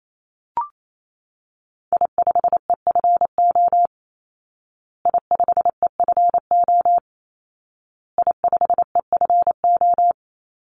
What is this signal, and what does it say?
Morse code, heard as a keyed tone, about 28 wpm, 700 Hz. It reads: I5EFO I5EFO I5EFO
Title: Call Signs Repeated 2x - 28wpm